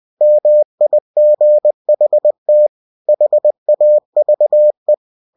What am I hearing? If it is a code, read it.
Morse code, heard as a keyed tone, about 20 wpm, 600 Hz. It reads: MIGHT HAVE
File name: cw-books / winnie-the-pooh / winnie-20wpm-page-0108